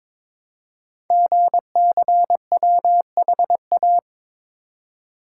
Morse code, heard as a keyed tone, about 22 wpm, 700 Hz. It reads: GCWHA
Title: Code Groups - 5 Letter:Numbers - 22wpm